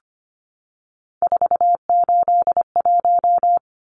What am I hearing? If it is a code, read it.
Morse code, heard as a keyed tone, about 25 wpm, 700 Hz. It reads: 481
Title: Numbers - 3-Digit 25wpm